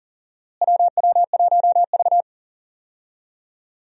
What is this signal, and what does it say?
Morse code, heard as a keyed tone, about 38 wpm, 700 Hz. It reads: WW1V
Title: USA Call Signs - 4 Characters - Repeated 2x - 40wpm